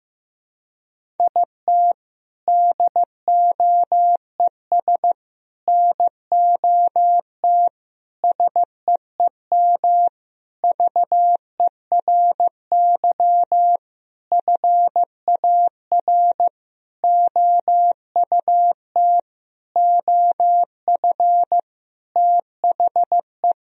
Morse code, heard as a keyed tone, about 16 wpm, 700 Hz. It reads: IT DOES NOT SEEM VERY FAR OUT OF THE
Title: Sentences from Top 1000 Words - Review Part 1 - 15wpm